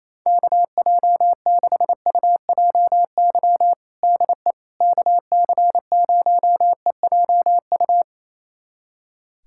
Morse code, heard as a keyed tone, about 27 wpm, 700 Hz. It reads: KJ6UJY DE KC0EJU
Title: USA Callsign DE Callsign - 6 Characters - 28wpm